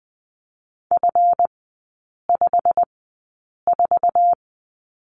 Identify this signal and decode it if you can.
Morse code, heard as a keyed tone, about 20 wpm, spaced as for 8 wpm, 700 Hz. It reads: F54
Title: 191 Lesson 31e - Letter v - Three Characters 20wpm